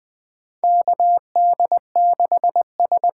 Morse code, heard as a keyed tone, about 20 wpm, 700 Hz. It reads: KD6S
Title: USA Call Signs - 6 Characters - 8x Character Spacing - 20wpm